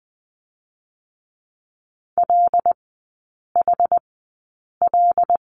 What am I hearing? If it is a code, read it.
Morse code, heard as a keyed tone, about 20 wpm, spaced as for 8 wpm, 700 Hz. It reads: LHL
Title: 081 Lesson 14e - Number 2 - Three Characters 20wpm